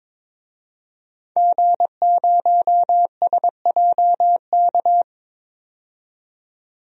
Morse code, heard as a keyed tone, about 22 wpm, 700 Hz. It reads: G0SJK
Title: DXCC-100 Call Signs - 5 Characters - 22wpm